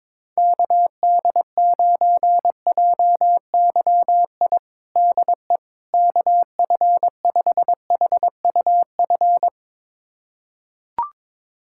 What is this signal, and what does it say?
Morse code, heard as a keyed tone, about 22 wpm, 700 Hz. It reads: KD9JYI DE KF5HUF
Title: USA Callsign DE Callsign 22wpm